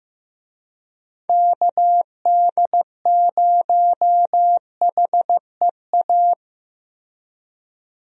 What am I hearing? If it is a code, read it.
Morse code, heard as a keyed tone, about 15 wpm, 700 Hz. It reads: KD0HEA
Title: USA Call Signs - 6 Characters - 15wpm